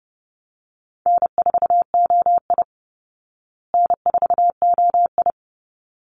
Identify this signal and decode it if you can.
Morse code, heard as a keyed tone, about 30 wpm, 700 Hz. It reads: N4OI N4OI
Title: Easy Call Signs Repeated 2x - 30wpm